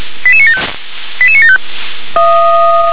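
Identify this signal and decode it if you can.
Five-tone selective call (ZVEI), heard as quick runs of five tones, tones of about 80 ms. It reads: 88975 88975
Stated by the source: ZVEI five-tone call received on SDR.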